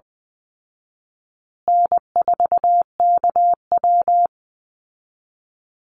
Morse code, heard as a keyed tone, about 20 wpm, 700 Hz. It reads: N4KW
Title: Easy Call Signs Repeated 3x - 20wpm